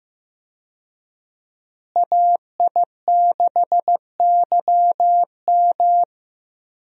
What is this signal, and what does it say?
Morse code, heard as a keyed tone, about 15 wpm, 700 Hz. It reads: AI6YM
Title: USA Call Signs - 5 Characters - Repeated 3x - 15wpm